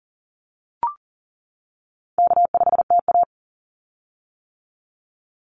Morse code, heard as a keyed tone, about 40 wpm, 700 Hz. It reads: K5TA
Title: Easy Call Signs - 40wpm